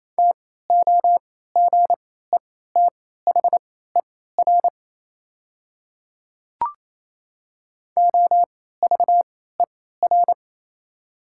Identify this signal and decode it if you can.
Morse code, heard as a keyed tone, about 27 wpm, spaced as for 16 wpm, 690 Hz. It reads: TOGETHER OVER
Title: 3x Character Spacing - Top 300 Words - 28wpm